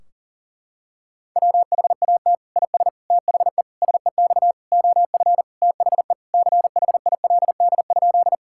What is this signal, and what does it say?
Morse code, heard as a keyed tone, about 40 wpm, 700 Hz. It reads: WHAT IS THE SEX OF THE CHILD?
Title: Sentences from Top 1600 Words 40wpm